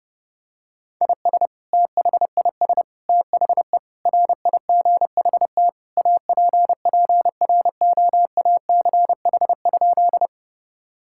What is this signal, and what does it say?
Morse code, heard as a keyed tone, about 30 wpm, 700 Hz. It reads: IS THIS THE RIGHT APPROACH?
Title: Sentences from Top 1000 Words - Review Part 6 - 30wpm